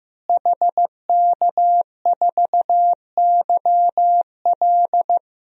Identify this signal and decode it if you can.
Morse code, heard as a keyed tone, about 15 wpm, 700 Hz. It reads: HK4YL
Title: DXCC-100 Call Signs - 5 Characters - Repeated 3x - 15wpm